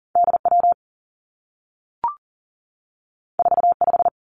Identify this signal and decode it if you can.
Morse code, heard as a keyed tone, about 39 wpm, 700 Hz. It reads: D W 4 5
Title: Two Challenging Letter-Number Combinations 40wpm